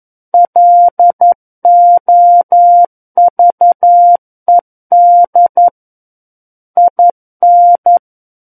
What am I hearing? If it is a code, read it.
Morse code, heard as a keyed tone, about 11 wpm, 700 Hz. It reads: LOVED IN
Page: War of the Worlds - 02-Chapter 2 - 11 WPM